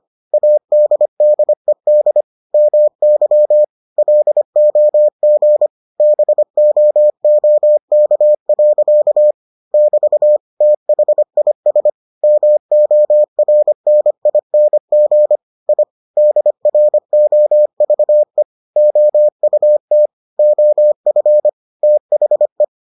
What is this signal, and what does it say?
Morse code, heard as a keyed tone, about 25 wpm, 600 Hz. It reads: ADDED MY LOG BOOK. <BT> THIS MORNING I DROVE OUT OF THE